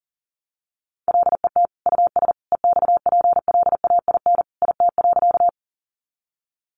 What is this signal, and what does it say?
Morse code, heard as a keyed tone, about 40 wpm, 700 Hz. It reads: LET US EXPLAIN IT.
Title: Sentences from Top 1100 Words 40wpm